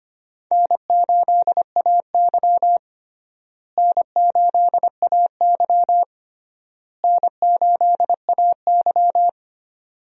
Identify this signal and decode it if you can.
Morse code, heard as a keyed tone, about 25 wpm, 700 Hz. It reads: N8AY N8AY N8AY
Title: USA Call Signs - 4 Characters - Repeated 3x - 25wpm